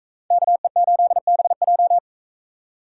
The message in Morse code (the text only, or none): KE8BJ